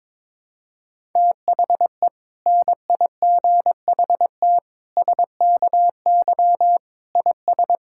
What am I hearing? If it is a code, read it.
Morse code, heard as a keyed tone, about 22 wpm, 700 Hz. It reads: THE NIGHT SKY IS
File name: Difficult Sentences with Hints - 22wpm